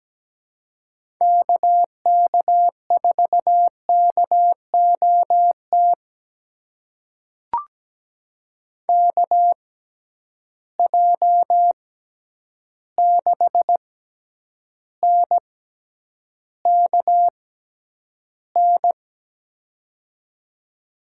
Morse code, heard as a keyed tone, about 17 wpm, spaced as for 9 wpm, 700 Hz. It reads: KK4KOT KJ6NKN
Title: USA Call Signs - 6 Characters - 6x Character Spacing - 17wpm